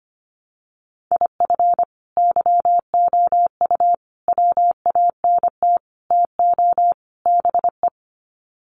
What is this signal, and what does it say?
Morse code, heard as a keyed tone, about 25 wpm, 700 Hz. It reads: IF YOU WANT TO BE
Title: Sets of 5 Words Encore 25wpm